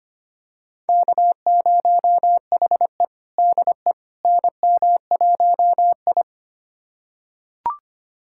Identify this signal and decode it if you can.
Morse code, heard as a keyed tone, about 25 wpm, 700 Hz. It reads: K0HE DE NM1I
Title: USA Callsign DE Callsign - 4 Characters - 25wpm